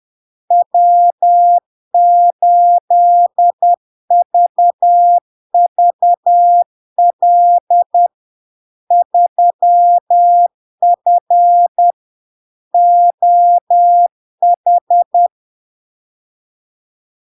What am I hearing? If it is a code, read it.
Morse code, heard as a keyed tone, about 10 wpm, 700 Hz. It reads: W8VVL 3F OH
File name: LARK-FD-2022-Log-in-CW-10wpm-700-Hz